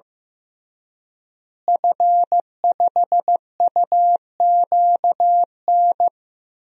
Morse code, heard as a keyed tone, about 15 wpm, 700 Hz. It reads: F5UQN
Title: DXCC-100 Call Signs - 5 Characters - Repeated 3x - 15wpm